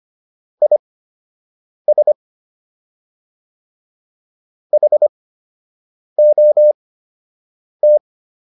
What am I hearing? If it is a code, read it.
Morse code, heard as a keyed tone, about 25 wpm, spaced as for 7 wpm, 600 Hz. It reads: IS HOT